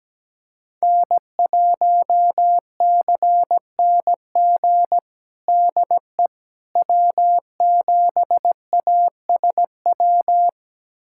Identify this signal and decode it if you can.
Morse code, heard as a keyed tone, about 17 wpm, 700 Hz. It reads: N1CNG DE W7ASW
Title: USA Callsign DE Callsign - 5 Characters - 17wpm